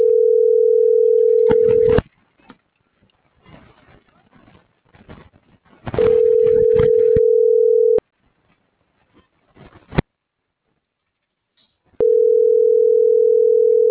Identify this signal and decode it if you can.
Ringback tone (North America), 440 + 480 Hz.